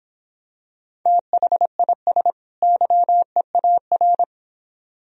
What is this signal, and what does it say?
Morse code, heard as a keyed tone, about 25 wpm, 700 Hz. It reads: THIS YEAR